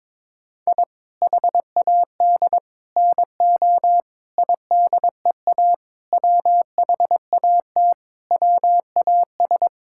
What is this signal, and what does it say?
Morse code, heard as a keyed tone, about 21 wpm, 700 Hz. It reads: I HAD NO IDEA WHAT WAS
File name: practice / 22wpm / Sentences with 8-Word Phrases - 22wpm